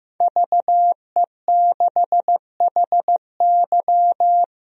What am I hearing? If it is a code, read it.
Morse code, heard as a keyed tone, about 15 wpm, 700 Hz. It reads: VE6HY
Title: DXCC-100 Call Signs - 5 Characters - 15wpm